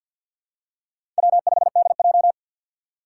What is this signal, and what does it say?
Morse code, heard as a keyed tone, about 50 wpm, 700 Hz. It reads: W5DJ
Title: Four Challenging Letter-Number Combinations 50wpm